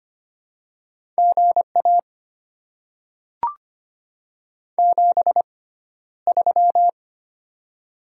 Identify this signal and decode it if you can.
Morse code, heard as a keyed tone, about 25 wpm, spaced as for 13 wpm, 700 Hz. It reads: GA 73
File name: Top 20 QSO Elements - 6x Character Spacing - 25wpm